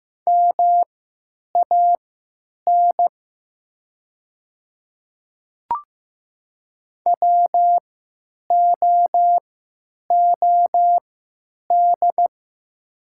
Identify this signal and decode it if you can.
Morse code, heard as a keyed tone, about 15 wpm, spaced as for 9 wpm, 700 Hz. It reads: MAN WOOD
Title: 3x Character Spacing - Top 300 Words - 15wpm